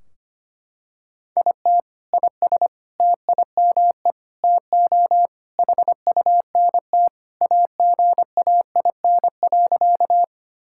Morse code, heard as a keyed tone, about 25 wpm, 700 Hz. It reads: IT IS TIME TO HUNT AGAIN.
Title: Sentences from Top 1600 Words 25wpm